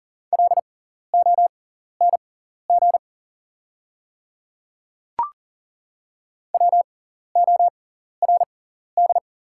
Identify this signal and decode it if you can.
Morse code, heard as a keyed tone, about 39 wpm, spaced as for 14 wpm, 700 Hz. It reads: LONG WORD
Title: 6x Character Spacing - Top 300 Words - 40wpm